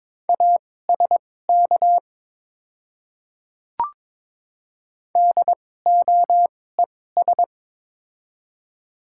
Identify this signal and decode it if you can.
Morse code, heard as a keyed tone, about 22 wpm, spaced as for 16 wpm, 700 Hz. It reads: ASK DOES
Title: 2x Character Spacing - Top 200 Words - 22wpm